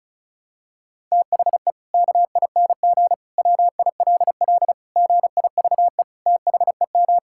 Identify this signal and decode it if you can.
Morse code, heard as a keyed tone, about 36 wpm, 700 Hz. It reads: THE KING WILL GIVE THEM
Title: Sentences from Top 1000 Words - Review Part 5 - 35wpm